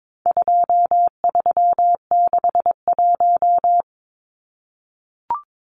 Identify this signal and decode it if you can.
Morse code, heard as a keyed tone, about 22 wpm, 700 Hz. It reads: 2361
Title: Numbers - 4-Digit 22wpm